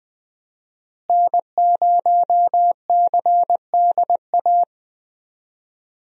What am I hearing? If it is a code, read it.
Morse code, heard as a keyed tone, about 21 wpm, 700 Hz. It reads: N0CDA
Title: USA Call Signs - 5 Characters - Repeated 3x - 20wpm